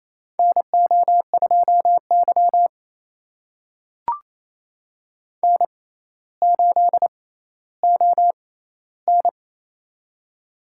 Morse code, heard as a keyed tone, about 28 wpm, spaced as for 14 wpm, 700 Hz. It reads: NO2Y N8ON
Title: USA Call Signs - 6x Character Spacing - 28wpm